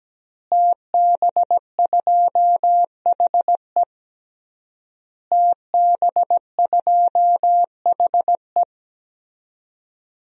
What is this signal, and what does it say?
Morse code, heard as a keyed tone, about 17 wpm, 700 Hz. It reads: TB2HE TB2HE
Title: DXCC-100 Call Signs - 5 Characters - Repeated 3x - 17wpm